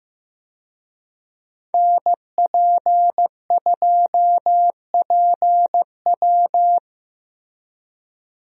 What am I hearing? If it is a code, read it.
Morse code, heard as a keyed tone, about 15 wpm, 700 Hz. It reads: NP2PW
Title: USA Call Signs - 5 Characters - Repeated 3x - 15wpm